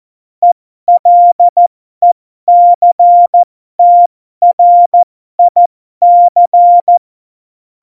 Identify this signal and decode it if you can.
Morse code, heard as a keyed tone, about 14 wpm, spaced as for 12 wpm, 700 Hz. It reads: ELECTRIC
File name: Morsekurs / Witze